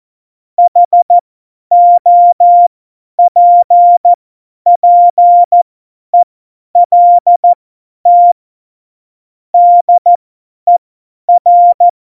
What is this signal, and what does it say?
Morse code, heard as a keyed tone, about 14 wpm, spaced as for 10 wpm, 700 Hz. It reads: HOPPELT DER
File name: Morsekurs / Witze / Witz 03. November 2020, 50 F70